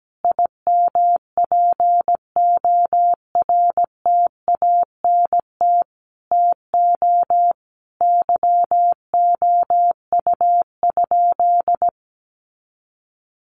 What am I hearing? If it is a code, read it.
Morse code, heard as a keyed tone, about 17 wpm, 700 Hz. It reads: IMPORTANT TO YOU?
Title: Sentences from Top 1300 Words 17wpm